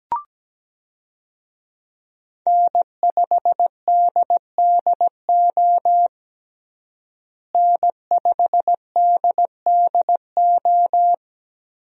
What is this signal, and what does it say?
Morse code, heard as a keyed tone, about 17 wpm, 700 Hz. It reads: N5DDO N5DDO
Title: USA Call Signs - 5 Characters - Repeated 3x - 17wpm